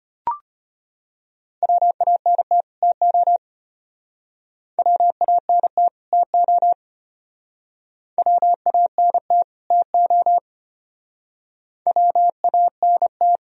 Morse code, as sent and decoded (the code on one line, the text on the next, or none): .-- .- -. - / - --- / .-- .- -. - / - --- / .-- .- -. - / - --- / .-- .- -. -
WANT TO WANT TO WANT TO WANT